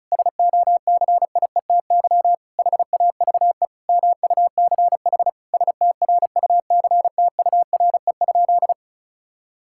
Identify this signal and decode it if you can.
Morse code, heard as a keyed tone, about 36 wpm, 700 Hz. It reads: SOCIETY HAVE MUCH STRUCTURE?
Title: Sentences from Top 1000 Words - Review Part 6 - 35wpm